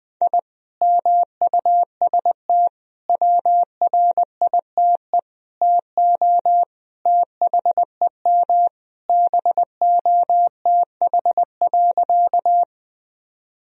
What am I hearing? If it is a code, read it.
Morse code, heard as a keyed tone, about 21 wpm, 700 Hz. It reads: I MUST WRITE TO THEM BOTH.
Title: Sentences from Top 1000 Words - Review Part 1 - 20wpm